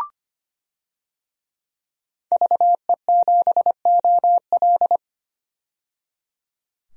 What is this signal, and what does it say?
Morse code, heard as a keyed tone, about 25 wpm, 700 Hz. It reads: VE7OL
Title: DXCC-100 Call Signs - 5 Characters - 25wpm